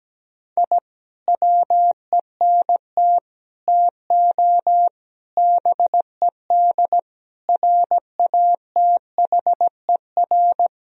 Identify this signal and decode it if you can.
Morse code, heard as a keyed tone, about 17 wpm, 700 Hz. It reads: I WENT TO BED RATHER
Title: Sentences from Top 1000 Words - Review Part 4 - 17wpm